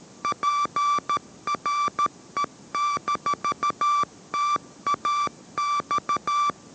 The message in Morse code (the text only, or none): PRE-TAX